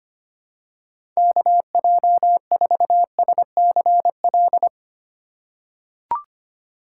Morse code, heard as a keyed tone, about 25 wpm, 700 Hz. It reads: KJ4SCL
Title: USA Call Signs - 6 Characters - 3x Character Spacing - 25wpm